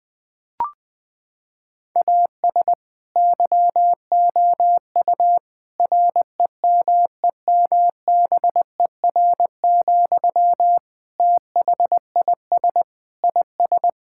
Morse code, heard as a keyed tone, about 20 wpm, 700 Hz. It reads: AS YOU REMEMBER, THIS IS
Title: Sentences from Top 1000 Words - Review Part 5 - 20wpm